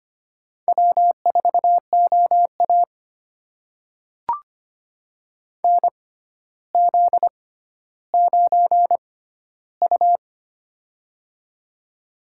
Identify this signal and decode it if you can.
Morse code, heard as a keyed tone, about 25 wpm, spaced as for 13 wpm, 700 Hz. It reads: W4OA NZ9U
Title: USA Call Signs - 4 Characters - 6x Character Spacing - 25wpm